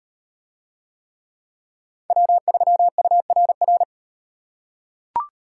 Morse code, heard as a keyed tone, about 38 wpm, 700 Hz. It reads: W3URR